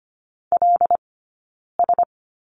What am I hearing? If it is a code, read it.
Morse code, heard as a keyed tone, about 25 wpm, 700 Hz. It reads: L S
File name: Lesson 17d - Letter u - Two Characters 25wpm